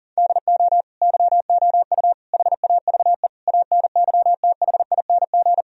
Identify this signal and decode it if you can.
Morse code, heard as a keyed tone, about 40 wpm, 700 Hz. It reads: DO YOU HAVE ANYTHING
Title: Sentences from Top 1000 Words - Review Part 1 - 40wpm